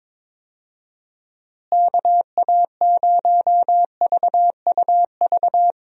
Morse code, heard as a keyed tone, about 22 wpm, 700 Hz. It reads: KA0VUV
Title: USA Call Signs - 6 Characters - Repeated 2x - 22wpm